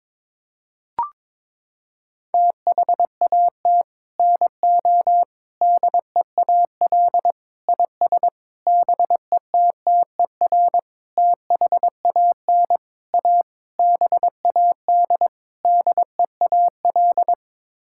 Morse code, heard as a keyed tone, about 22 wpm, 700 Hz. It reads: THAT NO DEAL IS BETTER THAN A BAD DEAL
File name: Sets of 9 Words - 22wpm